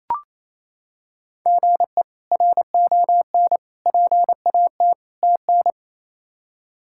Morse code, heard as a keyed tone, about 28 wpm, 700 Hz. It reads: GE RON PAT TN
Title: K1USN SST QSO Practice - 28wpm